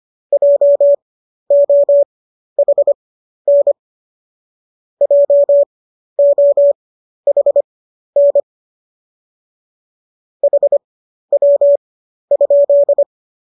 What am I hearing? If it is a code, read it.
Morse code, heard as a keyed tone, about 25 wpm, spaced as for 12 wpm, 570 Hz. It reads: JOHN JOHN HW?